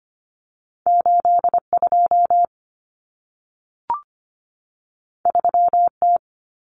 Morse code, heard as a keyed tone, about 25 wpm, 700 Hz. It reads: 82 3T
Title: Common Cut Numbers - 2-Digit 25wpm